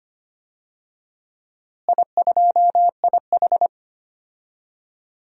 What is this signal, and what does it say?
Morse code, heard as a keyed tone, about 25 wpm, 700 Hz. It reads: I2IH
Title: DXCC-100 Call Signs - 4 Characters - Repeated 2x - 25wpm